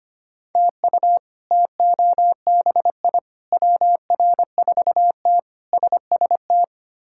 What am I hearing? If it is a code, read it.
Morse code, heard as a keyed tone, about 25 wpm, 700 Hz. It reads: TU TOBI WR4T SST
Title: K1USN SST QSO Practice - 25wpm